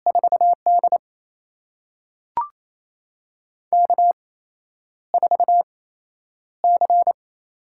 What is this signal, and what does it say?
Morse code, heard as a keyed tone, about 28 wpm, spaced as for 10 wpm, 700 Hz. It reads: 4D K4C
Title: USA Call Signs - 8x Character Spacing - 28wpm